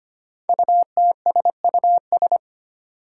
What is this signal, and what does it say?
Morse code, heard as a keyed tone, about 25 wpm, 700 Hz. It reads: UTSUS
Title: Five Challenging Letter-Number Combinations 25wpm